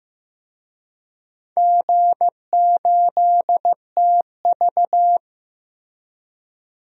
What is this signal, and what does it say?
Morse code, heard as a keyed tone, about 16 wpm, 700 Hz. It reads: G8TV
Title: DXCC-100 Call Signs - 4 Characters - Repeated 3x - 15wpm